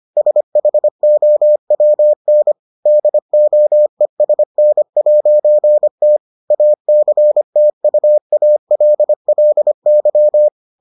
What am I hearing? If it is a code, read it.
Morse code, heard as a keyed tone, about 25 wpm, 600 Hz. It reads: SHOWN DOESN'T ACTUALLY